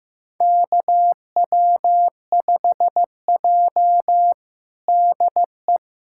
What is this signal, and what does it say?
Morse code, heard as a keyed tone, about 15 wpm, 700 Hz. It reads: KW5J DE
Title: USA Callsign DE Callsign - 4 Characters - 15wpm